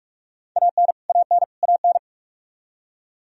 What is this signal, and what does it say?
Morse code, heard as a keyed tone, about 46 wpm, 700 Hz. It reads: AN AN AN
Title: Top 100 Words Repeated 45wpm